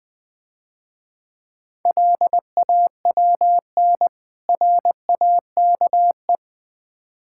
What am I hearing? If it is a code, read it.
Morse code, heard as a keyed tone, about 23 wpm, 700 Hz. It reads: LAWN RAKE
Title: Easy Words - Speed-Racing with Hints - 20wpm